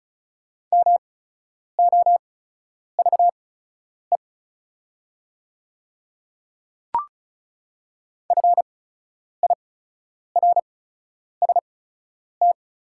Morse code, heard as a keyed tone, about 36 wpm, spaced as for 10 wpm, 690 Hz. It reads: MOVE FIRST
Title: 8x Character Spacing - Top 200 Words - 35wpm